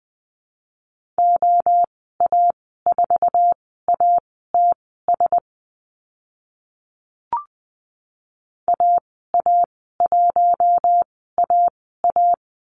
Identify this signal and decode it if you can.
Morse code, heard as a keyed tone, about 20 wpm, spaced as for 14 wpm, 700 Hz. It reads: OA4ATS AA1AA
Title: 088 Lesson 15g - Number 5 - Call Signs 20wpm